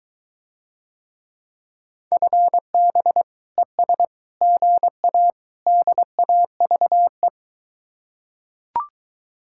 Morse code, heard as a keyed tone, about 23 wpm, 700 Hz. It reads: FB ES GA DAVE